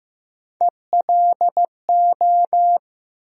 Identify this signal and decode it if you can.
Morse code, heard as a keyed tone, about 15 wpm, 700 Hz. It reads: ELO
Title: USA Call Sign Postfixes - 3 Letters 15wpm